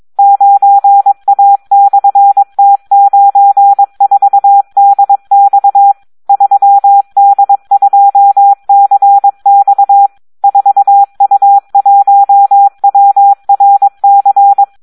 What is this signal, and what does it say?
Morse code, heard as a keyed tone, about 22 wpm, 800 Hz, with radio noise. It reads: 9A/T94DX 3D2CX 4U1WRC
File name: Audio practica CW / SD 1 22